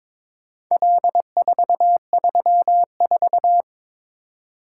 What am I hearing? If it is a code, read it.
Morse code, heard as a keyed tone, about 22 wpm, 700 Hz. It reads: L434
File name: practice / 22wpm / Four Letter-Number Combinations - 22wpm